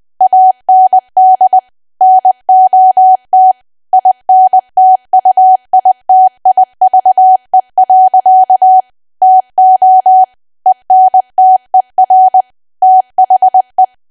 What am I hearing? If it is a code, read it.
Morse code, heard as a keyed tone, about 20 wpm, 750 Hz. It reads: AND NOT INTUITIVE. TO ENTER THE
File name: Archive / 20 WPM / 20WPM